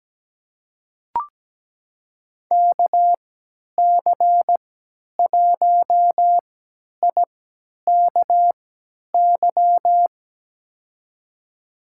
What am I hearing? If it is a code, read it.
Morse code, heard as a keyed tone, about 17 wpm, spaced as for 10 wpm, 700 Hz. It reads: KC1IKY
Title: USA Call Signs - 6 Characters - 3x Character Spacing - 17wpm